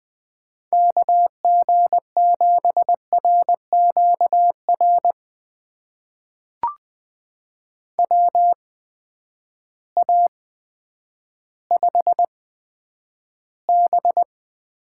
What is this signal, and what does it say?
Morse code, heard as a keyed tone, about 20 wpm, spaced as for 10 wpm, 700 Hz. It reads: KG7RQR WA5B
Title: USA Call Signs - 6 Characters - 8x Character Spacing - 20wpm